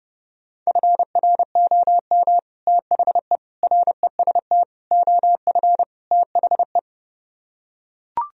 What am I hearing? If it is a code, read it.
Morse code, heard as a keyed tone, about 30 wpm, 700 Hz. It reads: FROM THE REST OF THE